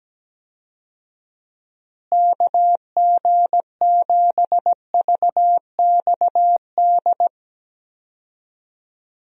Morse code, heard as a keyed tone, about 17 wpm, 700 Hz. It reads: KG7VXD
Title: USA Call Signs - 6 Characters - Repeated 3x - 17wpm